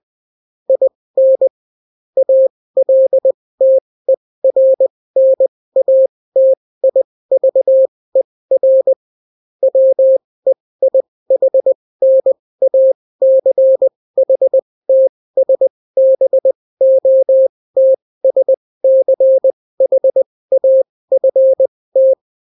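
Morse code, heard as a keyed tone, about 20 wpm, spaced as for 16 wpm, 530 Hz. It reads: IN ALTERNATIVER WEIHNACHTSBOTSCHAFT
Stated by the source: Daily Morse weather and top stories at 16 wpm, clean without QRN.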